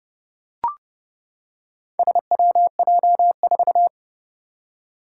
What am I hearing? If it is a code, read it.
Morse code, heard as a keyed tone, about 31 wpm, 700 Hz. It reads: SWJ4
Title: Four Challenging Letter-Number Combinations 30wpm